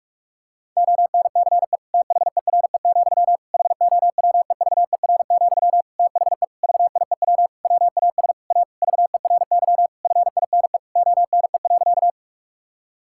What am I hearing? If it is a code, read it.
Morse code, heard as a keyed tone, about 41 wpm, 700 Hz. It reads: ONCE THERE, HOWEVER, THE VIEW WAS A VERY FINE ONE.
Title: Sentences from Top 1000 Words - Review Part 6 - 45wpm